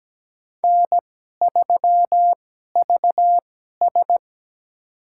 Morse code, heard as a keyed tone, about 17 wpm, spaced as for 12 wpm, 700 Hz. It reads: N3VS
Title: USA Call Signs - 4 Characters - 2x Character Spacing - 17wpm